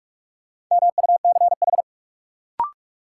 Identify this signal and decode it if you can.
Morse code, heard as a keyed tone, about 46 wpm, 700 Hz. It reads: MUCH
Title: Sentences from Top 200 Words - New Word Drill - 45wpm